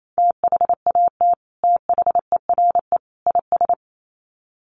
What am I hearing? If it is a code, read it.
Morse code, heard as a keyed tone, about 28 wpm, 700 Hz. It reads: THAT THERE IS
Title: Sets of 3 Words 28wpm